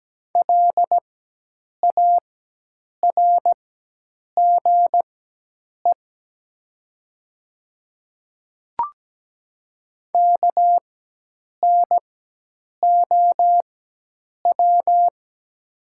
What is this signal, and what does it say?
Morse code, heard as a keyed tone, about 17 wpm, spaced as for 8 wpm, 700 Hz. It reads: LARGE KNOW
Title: 4x Character Spacing - Top 300 Words - 17wpm